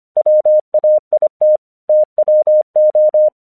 Morse code, heard as a keyed tone, about 25 wpm, 610 Hz. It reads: WAIT TWO
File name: 1 English 25wpm / AQC19